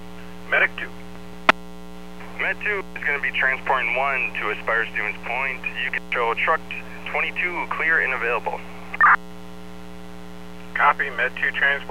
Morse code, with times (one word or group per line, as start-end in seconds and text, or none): none